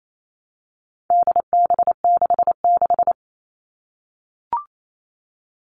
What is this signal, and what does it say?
Morse code, heard as a keyed tone, about 28 wpm, 700 Hz. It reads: DB66
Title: All Cut Numbers - 4-Digit 28wpm